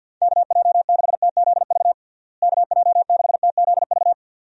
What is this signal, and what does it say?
Morse code, heard as a keyed tone, about 51 wpm, 700 Hz. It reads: KJ6TZV KJ6TZV
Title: USA Call Signs - 6 Characters - Repeated 2x - 50wpm